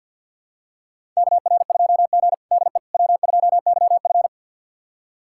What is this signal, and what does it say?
Morse code, heard as a keyed tone, about 53 wpm, 700 Hz. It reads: KR2G DE W2YF
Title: USA Callsign DE Callsign - 4 Characters - 50wpm